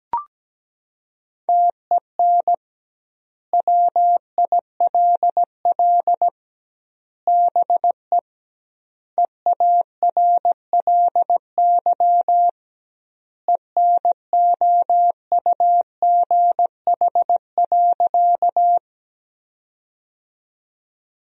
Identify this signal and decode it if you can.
Morse code, heard as a keyed tone, about 17 wpm, 700 Hz. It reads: TEN WILL BE EARLY ENOUGH.
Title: 2x Word-Spacing - Sentences from Top 400 Words - 17wpm